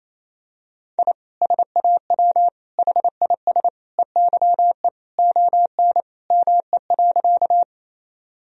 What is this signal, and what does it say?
Morse code, heard as a keyed tone, about 28 wpm, 700 Hz. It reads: I SAW HIS EYE ON ME.
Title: Sentences from Top 1000 Words - Review Part 1 - 28wpm